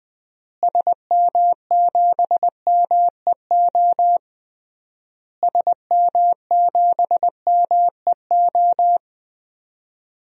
Morse code, heard as a keyed tone, about 20 wpm, 700 Hz. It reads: SM7MEO SM7MEO
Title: DXCC-100 Call Signs - 6 Characters - Repeated 3x - 20wpm